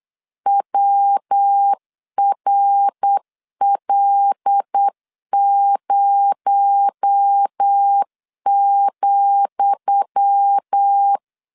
Morse code, heard as a keyed tone, about 8 wpm, 800 Hz. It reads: WRL0,
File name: LEZIONI / 10 SIGNS / 10D COMA 8wpm 3749